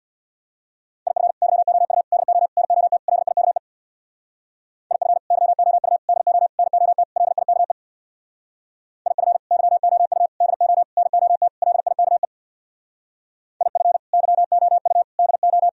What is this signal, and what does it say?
Morse code, heard as a keyed tone, about 65 wpm, 700 Hz. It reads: IF YOU DO NOT LIKE IF YOU DO NOT LIKE IF YOU DO NOT LIKE IF YOU DO